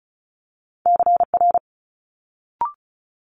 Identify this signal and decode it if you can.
Morse code, heard as a keyed tone, about 33 wpm, 700 Hz. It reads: CR
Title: Multi-letter Phonemes Quick Fire 35wpm